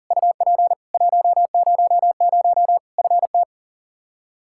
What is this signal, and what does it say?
Morse code, heard as a keyed tone, about 40 wpm, 700 Hz. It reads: UP 100 FT
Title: K1USN SST QSO Practice 40wpm